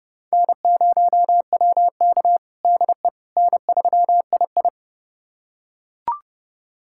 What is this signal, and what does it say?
Morse code, heard as a keyed tone, about 31 wpm, 700 Hz. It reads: N0WK DE N3II
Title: USA Callsign DE Callsign - 4 Characters - 30wpm